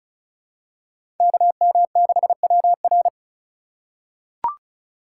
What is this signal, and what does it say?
Morse code, heard as a keyed tone, about 36 wpm, 700 Hz. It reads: KM6WR
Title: USA Call Signs - 5 Characters - 4x Character Spacing - 35wpm